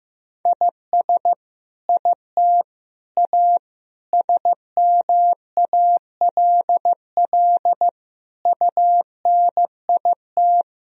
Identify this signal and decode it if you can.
Morse code, heard as a keyed tone, about 16 wpm, 700 Hz. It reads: IS IT A SMALL UNIT
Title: Sentences from Top 1000 Words - Review Part 2 - 15wpm